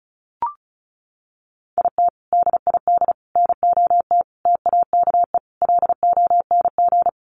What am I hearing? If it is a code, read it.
Morse code, heard as a keyed tone, about 34 wpm, 700 Hz. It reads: IT DID NOT TAKE LONG
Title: Sets of 5 Words Encore2 35wpm